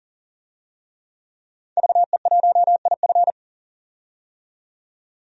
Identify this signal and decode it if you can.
Morse code, heard as a keyed tone, about 40 wpm, 700 Hz. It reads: VE1IF
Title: DXCC-100 Call Signs - 5 Characters - 40wpm